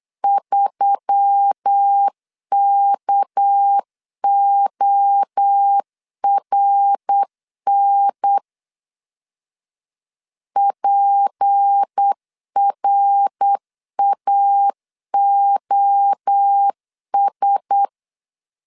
Morse code, heard as a keyed tone, about 8 wpm, 800 Hz. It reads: 3KORN PRAOS
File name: 8wpm